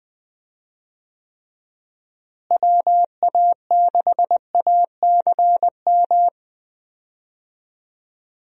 Morse code, heard as a keyed tone, about 20 wpm, 700 Hz. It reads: WA6ACM
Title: USA Call Signs - 6 Characters - Repeated 3x - 20wpm